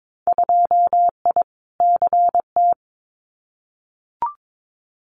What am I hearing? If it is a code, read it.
Morse code, heard as a keyed tone, about 22 wpm, 700 Hz. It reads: 2I CT
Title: Winter Field Day Exchanges 22wpm